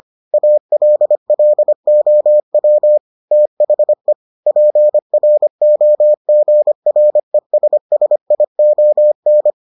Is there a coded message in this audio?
Yes: Morse code from 0.3 to 9.5 s, heard as a keyed tone, about 25 wpm, 600 Hz. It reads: ALLOW THE PROGRESSION